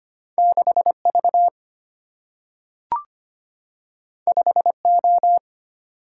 Morse code, heard as a keyed tone, about 25 wpm, 700 Hz. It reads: 6 V 5 O
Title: Single-Letter Single-Number 25wpm